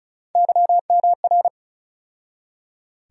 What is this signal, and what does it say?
Morse code, heard as a keyed tone, about 36 wpm, 700 Hz. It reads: YMR
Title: USA Call Sign Postfixes - 3 Letters 35wpm